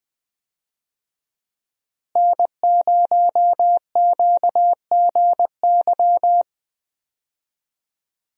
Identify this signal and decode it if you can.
Morse code, heard as a keyed tone, about 20 wpm, 700 Hz. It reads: N0QGY
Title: USA Call Signs - 5 Characters - Repeated 2x - 20wpm